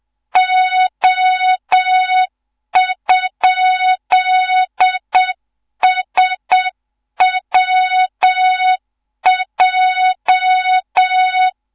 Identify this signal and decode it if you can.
Morse code, heard as a keyed tone, about 7 wpm, 3700 Hz. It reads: O?SWJ
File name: Morze-HA2MN / hang / 14 280 HA2MN